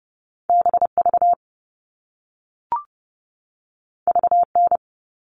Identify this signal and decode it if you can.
Morse code, heard as a keyed tone, about 30 wpm, 700 Hz. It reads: BV VN
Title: All Cut Numbers - 2-Digit 30wpm